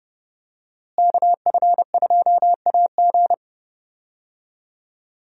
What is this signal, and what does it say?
Morse code, heard as a keyed tone, about 30 wpm, 700 Hz. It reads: KF2AG
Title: USA Call Signs - 5 Characters - Repeated 3x - 30wpm